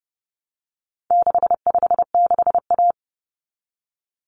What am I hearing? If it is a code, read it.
Morse code, heard as a keyed tone, about 30 wpm, 700 Hz. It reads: 656A
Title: Common Cut Numbers - 4-Digit 30wpm